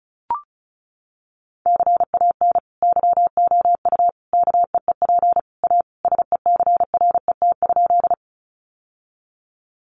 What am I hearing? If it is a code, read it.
Morse code, heard as a keyed tone, about 34 wpm, 700 Hz. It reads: CAN YOU KEEP A SECRET?
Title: Sentences from Top 1200 Words 35wpm